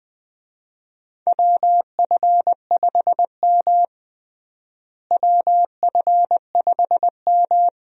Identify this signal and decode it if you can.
Morse code, heard as a keyed tone, about 20 wpm, 700 Hz. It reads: WF5M WF5M
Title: USA Call Signs - 4 Characters - Repeated 3x - 20wpm